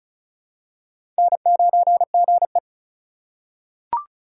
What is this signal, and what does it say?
Morse code, heard as a keyed tone, about 36 wpm, 700 Hz. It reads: N9GE
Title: USA Call Signs - 4 Characters - 3x Character Spacing - 35wpm